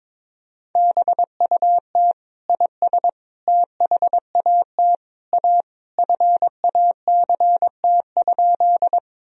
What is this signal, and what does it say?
Morse code, heard as a keyed tone, about 22 wpm, 700 Hz. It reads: BUT IS THAT A FACT?
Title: Sentences from Top 1000 Words - Review Part 3 - 22wpm